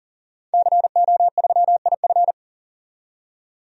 Morse code, heard as a keyed tone, about 40 wpm, 700 Hz. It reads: CO3IF
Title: DXCC-100 Call Signs - 5 Characters - 40wpm